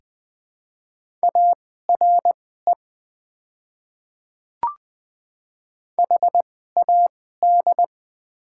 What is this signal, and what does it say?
Morse code, heard as a keyed tone, about 20 wpm, spaced as for 14 wpm, 700 Hz. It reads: ARE HAD
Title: 2x Character Spacing - Top 100 Words - 20wpm